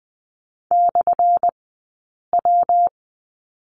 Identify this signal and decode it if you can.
Morse code, heard as a keyed tone, about 20 wpm, spaced as for 8 wpm, 700 Hz. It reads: /W
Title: 177 Lesson 29d - Slash - Two Characters 20wpm